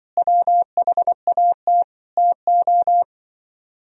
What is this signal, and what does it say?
Morse code, heard as a keyed tone, about 24 wpm, 700 Hz. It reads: WHAT TO
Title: Speed-Racing - Sets of 5 Words 20wpm